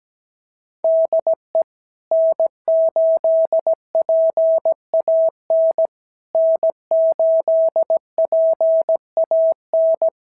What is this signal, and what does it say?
Morse code, heard as a keyed tone, about 17 wpm, 650 Hz. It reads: DE N8PAN N8PAN
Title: SOTA QSO Practice 17wpm